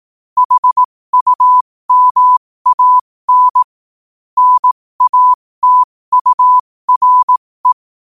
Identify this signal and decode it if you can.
Morse code, heard as a keyed tone, about 18 wpm, spaced as for 15 wpm, 1000 Hz. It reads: HUMAN NATURE
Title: 15 WPM morse code quotes for Fri, 01 Mar 2024